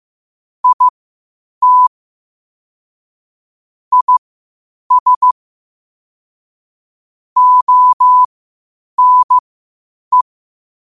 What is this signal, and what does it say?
Morse code, heard as a keyed tone, about 15 wpm, spaced as for 8 wpm, 1000 Hz. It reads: IT IS ONE
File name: qotdfiles / 8wpm